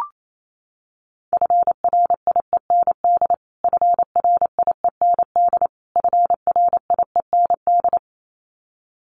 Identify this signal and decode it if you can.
Morse code, heard as a keyed tone, about 28 wpm, 700 Hz. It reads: FRIEND FRIEND FRIEND
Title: Top 500 Words Repeated 28wpm